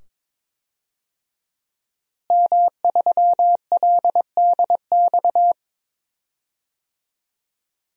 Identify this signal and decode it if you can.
Morse code, heard as a keyed tone, about 22 wpm, 700 Hz. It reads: M3LDX
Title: DXCC-100 Call Signs - 5 Characters - Repeated 2x - 22wpm